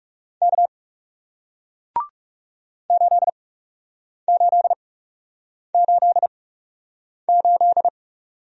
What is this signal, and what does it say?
Morse code, heard as a keyed tone, about 40 wpm, spaced as for 8 wpm, 700 Hz. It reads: K 8888